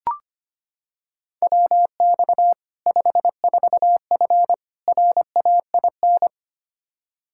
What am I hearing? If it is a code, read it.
Morse code, heard as a keyed tone, about 25 wpm, 700 Hz. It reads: WX 54F RAIN
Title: Practice QSO - Exchange 2 - Info - 25wpm